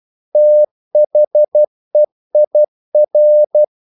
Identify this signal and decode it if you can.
Morse code, heard as a keyed tone, about 12 wpm, 600 Hz. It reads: THEIR
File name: CW / Practice / EnglishQuotations0312